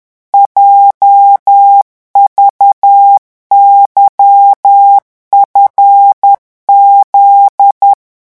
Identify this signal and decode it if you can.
Morse code, heard as a keyed tone, about 10 wpm, 800 Hz. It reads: JVYFZ
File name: morse mp3 / revision11 lettres vitesse 10